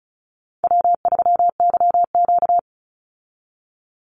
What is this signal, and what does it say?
Morse code, heard as a keyed tone, about 34 wpm, 700 Hz. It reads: W3YQ
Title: Call Signs Repeated 2x - 35wpm